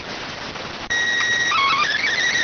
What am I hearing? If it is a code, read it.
SSTV (slow-scan television), PD 290.